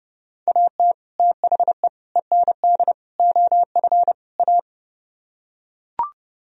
Morse code, heard as a keyed tone, about 32 wpm, 700 Hz. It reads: AT THE END OF A